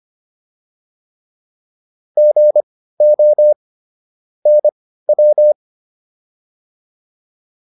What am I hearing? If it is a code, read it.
Morse code, heard as a keyed tone, about 25 wpm, spaced as for 15 wpm, 600 Hz. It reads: GO NW